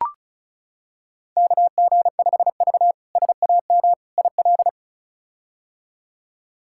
Morse code, heard as a keyed tone, about 34 wpm, 700 Hz. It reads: KG5V SAM IL
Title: K1USN SST QSO Practice - 35wpm